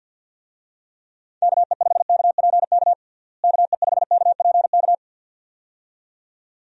Morse code, heard as a keyed tone, about 50 wpm, 700 Hz. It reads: KE5KPK KE5KPK
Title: USA Call Signs - 6 Characters - Repeated 2x - 50wpm